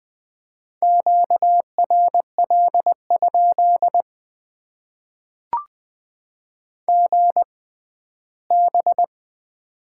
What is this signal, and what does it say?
Morse code, heard as a keyed tone, about 20 wpm, spaced as for 14 wpm, 700 Hz. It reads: QRL? GB
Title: Top 90 QSO Elements - 6x Character Spacing - 20wpm